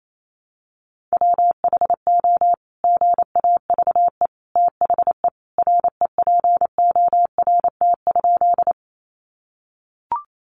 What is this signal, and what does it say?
Morse code, heard as a keyed tone, about 28 wpm, 700 Hz. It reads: WHO GAVE THE REPORT?
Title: Sentences from Top 900 Words 28wpm